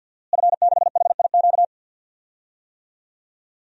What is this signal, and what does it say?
Morse code, heard as a keyed tone, about 52 wpm, 700 Hz. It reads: F6HIQ
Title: DXCC-100 Call Signs - 5 Characters - Repeated 2x - 50wpm